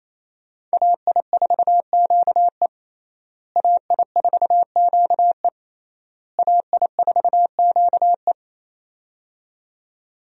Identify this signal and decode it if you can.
Morse code, heard as a keyed tone, about 27 wpm, 700 Hz. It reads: AI4QE AI4QE AI4QE
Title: USA Call Signs - 5 Characters - Repeated 3x - 28wpm